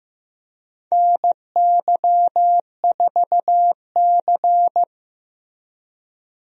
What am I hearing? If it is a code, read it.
Morse code, heard as a keyed tone, about 16 wpm, 700 Hz. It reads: NY4C
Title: USA Call Signs - 4 Characters - Repeated 3x - 15wpm